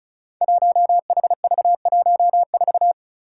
Morse code, heard as a keyed tone, about 36 wpm, 700 Hz. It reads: 1HV14
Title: Five Challenging Letter-Number Combinations 35wpm